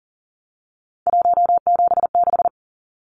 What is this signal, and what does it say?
Morse code, heard as a keyed tone, about 40 wpm, 700 Hz. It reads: 176
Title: Numbers - 3-Digit 40wpm